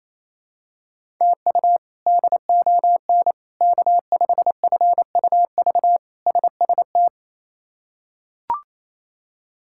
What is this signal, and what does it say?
Morse code, heard as a keyed tone, about 28 wpm, 700 Hz. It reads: TU DON K5FUV SST
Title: K1USN SST QSO Practice - 28wpm